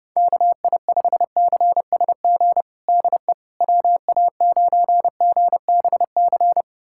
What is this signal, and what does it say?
Morse code, heard as a keyed tone, about 30 wpm, 700 Hz. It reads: KI5CSG DE WA9GBC
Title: USA Callsign DE Callsign - 6 Characters - 30wpm